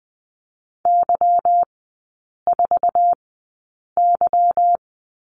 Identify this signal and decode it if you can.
Morse code, heard as a keyed tone, about 20 wpm, spaced as for 8 wpm, 690 Hz. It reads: Y4Y
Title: 145 Lesson 24e - Letter y - Three Characters 20wpm